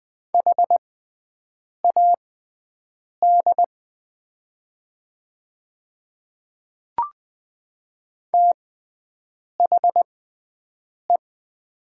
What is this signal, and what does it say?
Morse code, heard as a keyed tone, about 20 wpm, spaced as for 7 wpm, 700 Hz. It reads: HAD THE
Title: 6x Character Spacing - Top 100 Words - 20wpm